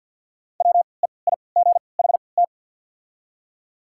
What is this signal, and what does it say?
Morse code, heard as a keyed tone, about 53 wpm, spaced as for 29 wpm, 700 Hz. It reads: WEIGHT
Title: 3x Character Spacing - Top 500 Words - 50wpm